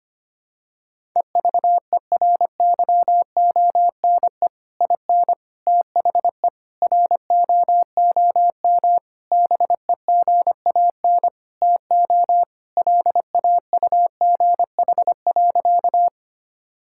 Morse code, heard as a keyed tone, about 25 wpm, 700 Hz. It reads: EVERYONE IN THE ROOM BEGAN TO LAUGH.
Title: Sentences from Top 1000 Words - Review Part 5 - 25wpm